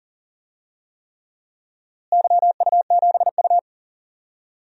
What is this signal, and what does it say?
Morse code, heard as a keyed tone, about 40 wpm, 700 Hz. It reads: YU7U
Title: DXCC-100 Call Signs - 4 Characters - Repeated 3x - 40wpm